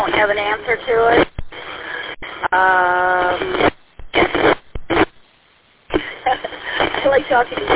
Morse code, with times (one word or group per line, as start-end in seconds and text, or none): none